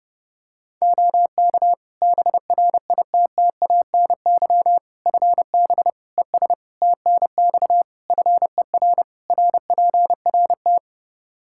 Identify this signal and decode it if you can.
Morse code, heard as a keyed tone, about 32 wpm, 700 Hz. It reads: OK BRITTANY FB ES TNX FER RPRT
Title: K1USN SST QSO Practice 30wpm